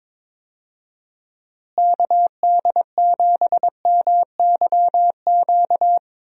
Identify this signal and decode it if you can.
Morse code, heard as a keyed tone, about 22 wpm, 700 Hz. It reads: KD7MYQ
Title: USA Call Signs - 6 Characters - Repeated 3x - 22wpm